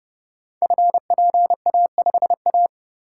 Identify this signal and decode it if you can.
Morse code, heard as a keyed tone, about 30 wpm, 700 Hz. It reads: FPA5A
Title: Code Groups - 5 Letter:Numbers - 30wpm